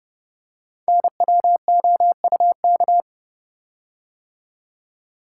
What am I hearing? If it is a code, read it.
Morse code, heard as a keyed tone, about 30 wpm, 700 Hz. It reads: NWOUK
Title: Code Groups - 5 Letters - 30wpm